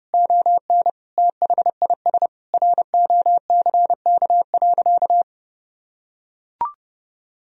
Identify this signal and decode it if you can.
Morse code, heard as a keyed tone, about 30 wpm, 700 Hz. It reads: ON THIS ROCK.
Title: Sentences from Top 1000 Words - Review Part 2 - 30wpm